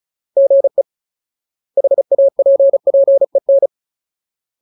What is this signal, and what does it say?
Morse code, heard as a keyed tone, about 34 wpm, 550 Hz. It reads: GE HAPPEN
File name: Ultimate Word List / 35WPM Ultimate